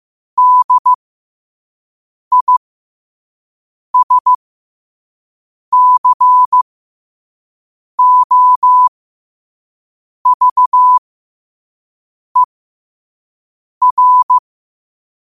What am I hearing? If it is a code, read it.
Morse code, heard as a keyed tone, about 15 wpm, spaced as for 5 wpm, 1000 Hz. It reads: DISCOVER
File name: qotdfiles / 5wpm